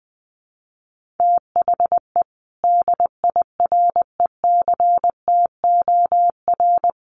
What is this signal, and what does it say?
Morse code, heard as a keyed tone, about 20 wpm, 700 Hz. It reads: THE DIRECTOR
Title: Sentences from Top 1300 Words 20wpm